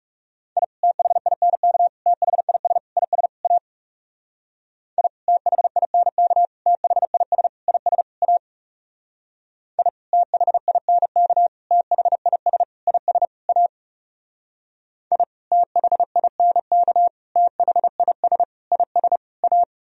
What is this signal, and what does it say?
Morse code, heard as a keyed tone, about 36 wpm, 700 Hz. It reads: I THINK THIS IS A I THINK THIS IS A I THINK THIS IS A I THINK THIS IS A
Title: Speed-Racing - Sets of 5 Words 30wpm